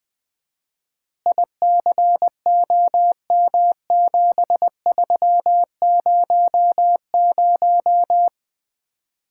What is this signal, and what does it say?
Morse code, heard as a keyed tone, about 20 wpm, 700 Hz. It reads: ICOM7300
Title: Common QSO Phrases - 4x Word-Spacing - 20wpm